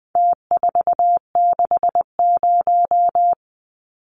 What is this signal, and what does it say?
Morse code, heard as a keyed tone, about 20 wpm, 700 Hz. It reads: T460
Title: Numbers - Contest Style - 4-Digit 20wpm